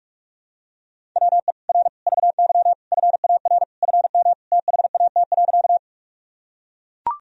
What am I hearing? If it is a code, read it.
Morse code, heard as a keyed tone, about 41 wpm, 700 Hz. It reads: WE R VY FAR FM THAT.
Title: Sentences from Top 300 Words and Abbreviations 45wpm